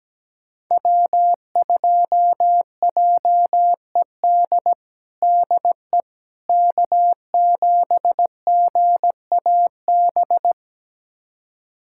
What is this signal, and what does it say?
Morse code, heard as a keyed tone, about 17 wpm, 700 Hz. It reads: W2JED DE K7GAB
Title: USA Callsign DE Callsign - 5 Characters - 17wpm